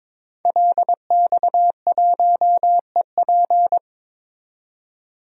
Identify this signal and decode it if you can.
Morse code, heard as a keyed tone, about 21 wpm, 700 Hz. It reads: LX1EP
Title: DXCC-100 Call Signs - 5 Characters - 22wpm